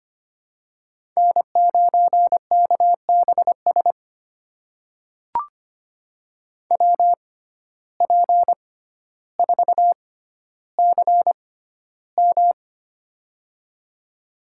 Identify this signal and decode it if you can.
Morse code, heard as a keyed tone, about 25 wpm, spaced as for 13 wpm, 700 Hz. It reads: N9KBS WP4CM
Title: USA Call Signs - 5 Characters - 6x Character Spacing - 25wpm